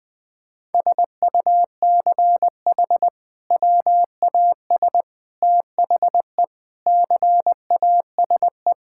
Morse code, heard as a keyed tone, about 20 wpm, 700 Hz. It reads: SUCH WAS THE CASE
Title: Sentences from Top 1600 Words 20wpm